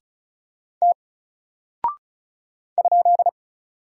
Morse code, heard as a keyed tone, about 34 wpm, 700 Hz. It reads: T ?
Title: Single Character (Letter:Number:Punctuation:Prosign) Alt 35wpm